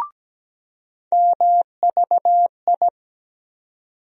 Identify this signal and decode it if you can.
Morse code, heard as a keyed tone, about 17 wpm, 700 Hz. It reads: MVI
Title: USA Call Sign Postfixes - 3 Letters 17wpm